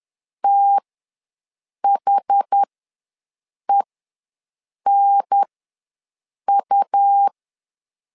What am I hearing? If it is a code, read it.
Morse code, heard as a keyed tone, about 10 wpm, spaced as for 6 wpm, 800 Hz. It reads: THENU
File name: LEZIONI / 2 UHDN / 2 UHDN ws 10wpm 2044